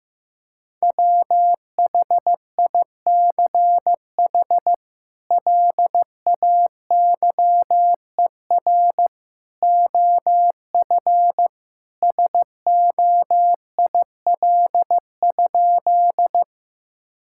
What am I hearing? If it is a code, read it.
Morse code, heard as a keyed tone, about 15 wpm, 700 Hz. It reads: WHICH LAYER OF SOIL?
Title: Sentences from Top 1600 Words 15wpm